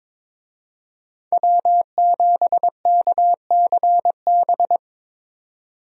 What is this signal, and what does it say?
Morse code, heard as a keyed tone, about 22 wpm, 700 Hz. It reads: W7KCB
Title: USA Call Signs - 5 Characters - 22wpm